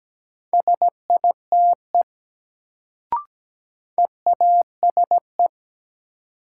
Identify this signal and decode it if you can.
Morse code, heard as a keyed tone, about 17 wpm, 700 Hz. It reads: SITE EASE
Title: Sentences from Top 1000 Words - Review Part 1 - New Word Drill - 17wpm